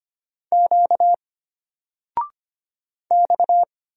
Morse code, heard as a keyed tone, about 25 wpm, 700 Hz. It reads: Q X
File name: Single-Letter-Number-25wpm